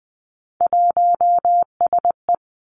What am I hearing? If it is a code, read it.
Morse code, heard as a keyed tone, about 20 wpm, 700 Hz. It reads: 1SE
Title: Three Challenging Letter-Number Combinations 20wpm